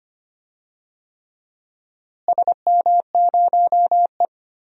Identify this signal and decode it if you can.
Morse code, heard as a keyed tone, about 25 wpm, 700 Hz. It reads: SM0E
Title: DXCC-100 Call Signs - 4 Characters - Repeated 3x - 25wpm